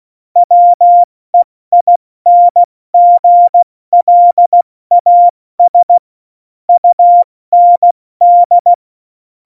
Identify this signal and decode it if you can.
Morse code, heard as a keyed tone, about 16 wpm, 700 Hz. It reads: WEINGLAS UND
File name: Morsekurs / Witze / Witz 18. September 2018, 70 F80